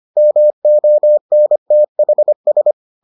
Morse code, heard as a keyed tone, about 25 wpm, 600 Hz. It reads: MONTHS